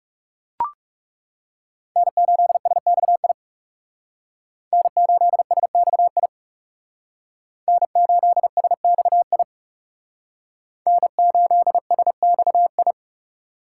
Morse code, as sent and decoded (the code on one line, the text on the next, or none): -. ---.. ... -..- .. / -. ---.. ... -..- .. / -. ---.. ... -..- .. / -. ---.. ... -..- ..
N8SXI N8SXI N8SXI N8SXI